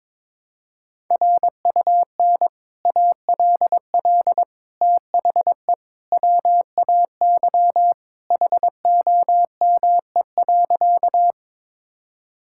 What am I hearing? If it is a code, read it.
Morse code, heard as a keyed tone, about 22 wpm, 700 Hz. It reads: RUN ALL THE WAY HOME.
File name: practice / 22wpm / Sentences with 3-Word Phrases - 22wpm